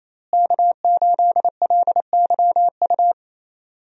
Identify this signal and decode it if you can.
Morse code, heard as a keyed tone, about 28 wpm, 700 Hz. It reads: K8LYU
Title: USA Call Signs - 5 Characters - Repeated 3x - 28wpm